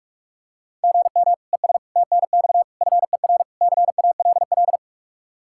Morse code, heard as a keyed tone, about 47 wpm, 700 Hz. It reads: GM ES TNX FER CALL
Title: Practice QSO - Exchange 1 - Beginning - 45wpm